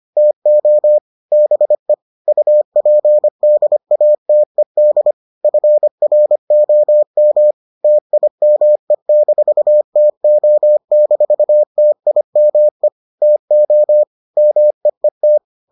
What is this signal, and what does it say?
Morse code, heard as a keyed tone, about 25 wpm, 600 Hz. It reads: TO BE UPDATED FROM TIME-TO-TIME TO MEET